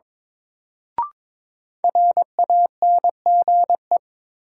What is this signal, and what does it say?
Morse code, heard as a keyed tone, about 22 wpm, 700 Hz. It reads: RANGE
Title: Sets of 4 Words - New Word Drill - 22wpm